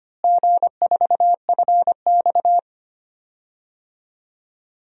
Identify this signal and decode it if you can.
Morse code, heard as a keyed tone, about 25 wpm, 700 Hz. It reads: G4FX
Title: DXCC-100 Call Signs - 4 Characters - Repeated 3x - 25wpm